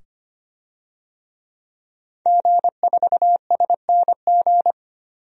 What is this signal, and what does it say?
Morse code, heard as a keyed tone, about 25 wpm, 700 Hz. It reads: G4SNG
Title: DXCC-100 Call Signs - 5 Characters - 25wpm